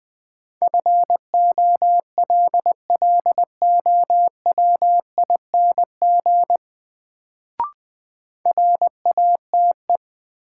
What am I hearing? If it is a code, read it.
Morse code, heard as a keyed tone, about 20 wpm, 700 Hz. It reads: FOLLOWING RATE
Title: Sentences from Top 700 Words - New Word Drill - 20wpm